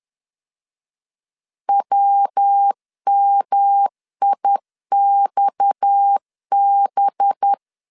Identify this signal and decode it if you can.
Morse code, heard as a keyed tone, about 10 wpm, 800 Hz. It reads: WMIXB